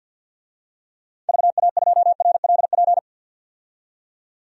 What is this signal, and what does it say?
Morse code, heard as a keyed tone, about 47 wpm, 700 Hz. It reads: VA2RLP
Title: DXCC-100 Call Signs - 6 Characters - 50wpm